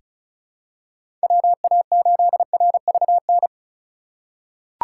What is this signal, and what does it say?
Morse code, heard as a keyed tone, about 36 wpm, 700 Hz. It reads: WA8RVN
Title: USA Call Signs - 6 Characters - 3x Character Spacing - 35wpm